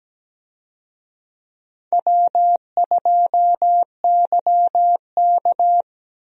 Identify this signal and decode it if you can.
Morse code, heard as a keyed tone, about 17 wpm, 700 Hz. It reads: W2YK
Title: USA Call Signs - 4 Characters - Repeated 2x - 17wpm